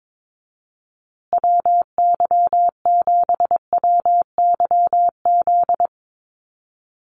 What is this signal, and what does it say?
Morse code, heard as a keyed tone, about 22 wpm, 700 Hz. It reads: WY7WYZ
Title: USA Call Signs Repeated 2x 22wpm